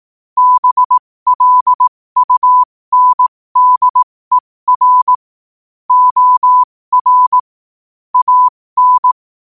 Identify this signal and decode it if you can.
Morse code, heard as a keyed tone, about 18 wpm, spaced as for 15 wpm, 1000 Hz. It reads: BLUNDER OR AN
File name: qotdfiles / 15wpm